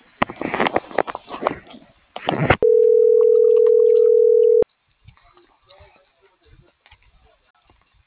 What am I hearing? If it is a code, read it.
Ringback tone (North America), 440 + 480 Hz.